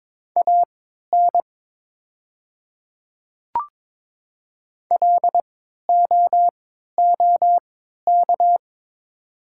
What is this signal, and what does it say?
Morse code, heard as a keyed tone, about 22 wpm, spaced as for 13 wpm, 700 Hz. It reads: AN LOOK
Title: 3x Character Spacing - Top 100 Words - 22wpm